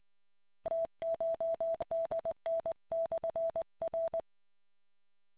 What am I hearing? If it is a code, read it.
Morse code, heard as a keyed tone, about 20 wpm, 660 Hz. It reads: TCHDN/R